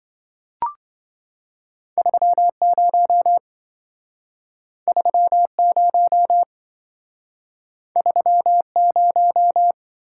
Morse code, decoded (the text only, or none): 30 30 30